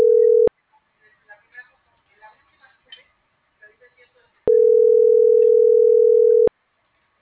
Ringback tone (North America), 440 + 480 Hz.